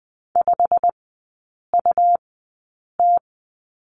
Morse code, heard as a keyed tone, about 20 wpm, spaced as for 8 wpm, 700 Hz. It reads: 5UT
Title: 100 Lesson 17e - Letter u - Three Characters 20wpm